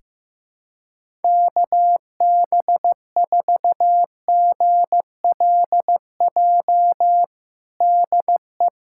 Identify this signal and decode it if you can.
Morse code, heard as a keyed tone, about 15 wpm, 700 Hz. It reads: KB4GLJ DE
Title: USA Callsign DE Callsign - 6 Characters - 15wpm